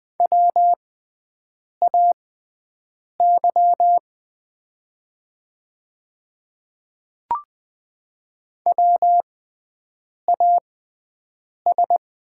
Morse code, heard as a keyed tone, about 20 wpm, spaced as for 7 wpm, 700 Hz. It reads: WAY WAS
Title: 6x Character Spacing - Top 100 Words - 20wpm